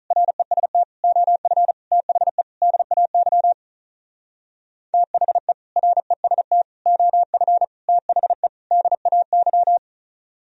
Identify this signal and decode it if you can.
Morse code, heard as a keyed tone, about 43 wpm, 700 Hz. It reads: REST OF THE DAY THE REST OF THE DAY
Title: Speed-Racing - Sets of 5 Words 35wpm